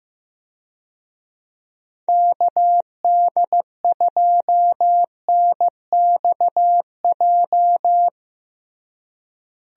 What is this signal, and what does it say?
Morse code, heard as a keyed tone, about 15 wpm, 700 Hz. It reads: KD2NXJ
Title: USA Call Signs - 6 Characters - Repeated 2x - 15wpm